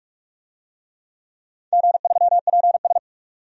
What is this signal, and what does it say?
Morse code, heard as a keyed tone, about 46 wpm, 700 Hz. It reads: G3PS
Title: DXCC-100 Call Signs - 4 Characters - 45wpm